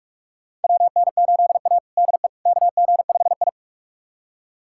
Morse code, heard as a keyed tone, about 44 wpm, 700 Hz. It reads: WN8A DE KG5I
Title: USA Callsign DE Callsign - 4 Characters - 45wpm